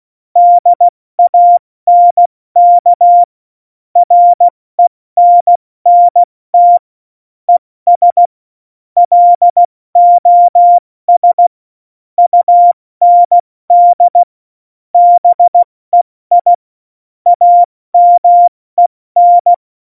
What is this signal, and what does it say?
Morse code, heard as a keyed tone, about 16 wpm, spaced as for 14 wpm, 700 Hz. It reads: DANK RENNT ES LOS UND BEI AMEN
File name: Morsekurs / Witze / Witz 26 Mai, 2020 70 F80